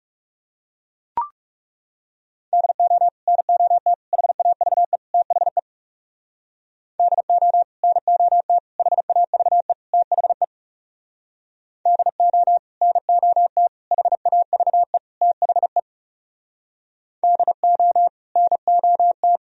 Morse code, heard as a keyed tone, about 37 wpm, 700 Hz. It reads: DO NOT HAVE THE DO NOT HAVE THE DO NOT HAVE THE DO NOT
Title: Speed-Racing - Sets of 4 Words 30wpm